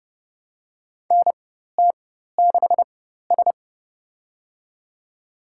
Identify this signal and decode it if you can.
Morse code, heard as a keyed tone, about 32 wpm, spaced as for 14 wpm, 700 Hz. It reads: NT6S
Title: USA Call Signs - 4 Characters - 4x Character Spacing - 30wpm